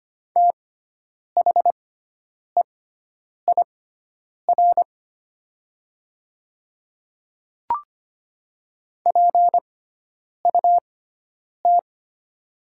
Morse code, heard as a keyed tone, about 25 wpm, spaced as for 9 wpm, 700 Hz. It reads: THEIR PUT
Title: 6x Character Spacing - Top 200 Words - 25wpm